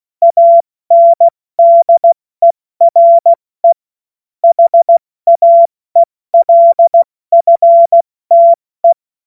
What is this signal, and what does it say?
Morse code, heard as a keyed tone, about 16 wpm, 680 Hz. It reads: ANDERE HAELFTE
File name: Morsekurs / Witze / Witz 18. Mai 2021, 70 F80